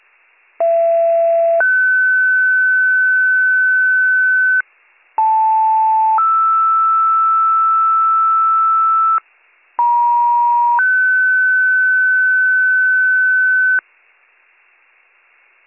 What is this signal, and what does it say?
Two-tone page, heard as a one-second tone, then a three-second tone, about 670 then 1500 Hz, about 880 then 1320 Hz, about 950 then 1500 Hz. It is heard from 0.6 to 13.8 s.